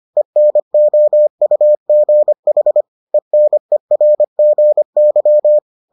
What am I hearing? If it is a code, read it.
Morse code, heard as a keyed tone, about 25 wpm, 600 Hz. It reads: ENOUGH ENERGY